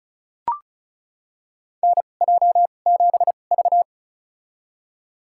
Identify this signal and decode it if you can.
Morse code, heard as a keyed tone, about 36 wpm, spaced as for 25 wpm, 700 Hz. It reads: NJ7V
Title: USA Call Signs - 2x Character Spacing - 35wpm